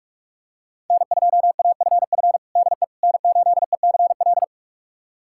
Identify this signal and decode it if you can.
Morse code, heard as a keyed tone, about 45 wpm, 700 Hz. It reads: N2AFF DE N8ECL
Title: USA Callsign DE Callsign - 5 Characters - 45wpm